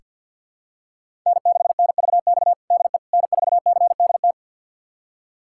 Morse code, heard as a keyed tone, about 48 wpm, 700 Hz. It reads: N6NVX DE N4CDT
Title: USA Callsign DE Callsign - 5 Characters - 50wpm